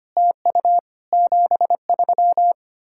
Morse code, heard as a keyed tone, about 32 wpm, 700 Hz. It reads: TU 73